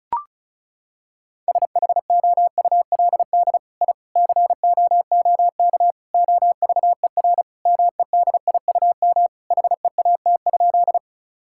Morse code, heard as a keyed tone, about 34 wpm, 700 Hz. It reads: SHOULD I COOK OVER MEDIUM HEAT?
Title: Sentences from Top 1000 Words - Review Part 6 - 35wpm